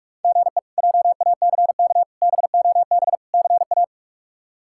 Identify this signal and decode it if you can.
Morse code, heard as a keyed tone, about 42 wpm, 700 Hz. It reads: GE JACK BOB CA
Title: K1USN SST QSO Practice - 45wpm